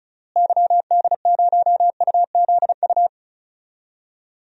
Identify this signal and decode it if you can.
Morse code, heard as a keyed tone, about 34 wpm, 700 Hz. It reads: YD0UZU
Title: DXCC-100 Call Signs - 6 Characters - Repeated 3x - 35wpm